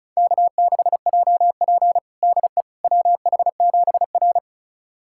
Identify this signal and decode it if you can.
Morse code, heard as a keyed tone, about 36 wpm, 700 Hz. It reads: K6JP DE WH7R
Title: USA Callsign DE Callsign - 4 Characters - 35wpm